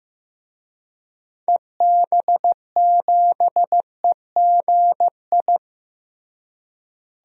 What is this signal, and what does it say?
Morse code, heard as a keyed tone, about 16 wpm, 700 Hz. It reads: EB7EGI
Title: DXCC-100 Call Signs - 6 Characters - Repeated 3x - 15wpm